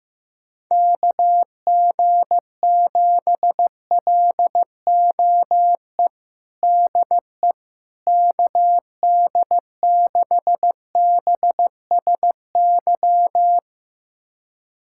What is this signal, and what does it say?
Morse code, heard as a keyed tone, about 15 wpm, 700 Hz. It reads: KG7LOE DE KD6BSY
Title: USA Callsign DE Callsign - 6 Characters - 15wpm